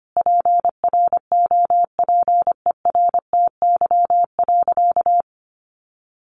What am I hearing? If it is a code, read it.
Morse code, heard as a keyed tone, about 24 wpm, 700 Hz. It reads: PROPERTY.
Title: Sentences from Top 1000 Words 25wpm